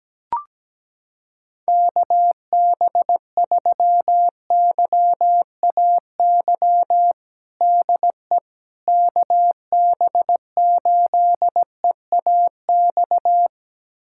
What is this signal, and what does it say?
Morse code, heard as a keyed tone, about 17 wpm, 700 Hz. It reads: KB3YAY DE KB8EAX
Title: USA Callsign DE Callsign - 6 Characters - 17wpm